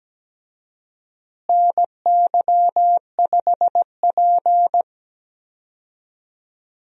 Morse code, heard as a keyed tone, about 17 wpm, 700 Hz. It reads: NY5P
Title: USA Call Signs - 4 Characters - Repeated 2x - 17wpm